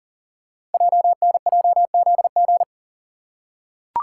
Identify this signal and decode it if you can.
Morse code, heard as a keyed tone, about 43 wpm, 700 Hz. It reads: JNJZG
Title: Code Groups - 5 Letters - 40wpm